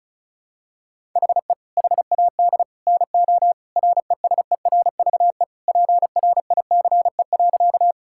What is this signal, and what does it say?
Morse code, heard as a keyed tone, about 36 wpm, 700 Hz. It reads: HE HAD NO RESERVE PRICE.
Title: Sentences from Top 1000 Words - Review Part 6 - 35wpm